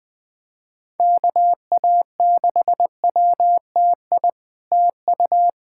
Morse code, heard as a keyed tone, about 21 wpm, 700 Hz. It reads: KA6WTI TU
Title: SOTA QSO Practice 20wpm